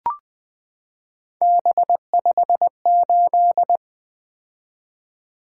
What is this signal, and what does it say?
Morse code, heard as a keyed tone, about 20 wpm, 700 Hz. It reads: B58
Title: Three Letter-Number Combinations - 20wpm